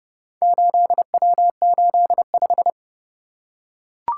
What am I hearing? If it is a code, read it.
Morse code, heard as a keyed tone, about 30 wpm, 700 Hz. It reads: 8W85